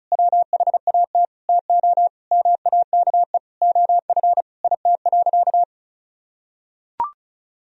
Morse code, heard as a keyed tone, about 36 wpm, 700 Hz. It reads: WHAT TO MAKE OF IT.
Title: Sentences from Top 1000 Words - Review Part 1 - 35wpm